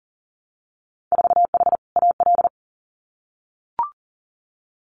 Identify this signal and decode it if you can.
Morse code, heard as a keyed tone, about 40 wpm, 700 Hz. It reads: 4H AL
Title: Winter Field Day Exchanges Encore 40wpm